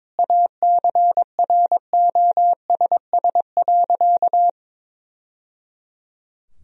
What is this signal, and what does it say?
Morse code, heard as a keyed tone, about 22 wpm, 700 Hz. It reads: ACROSS.
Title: Sentences from Top 1000 Words - Review Part 3 - 22wpm